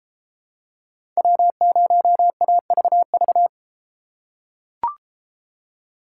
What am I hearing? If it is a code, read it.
Morse code, heard as a keyed tone, about 33 wpm, 700 Hz. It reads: W0AVV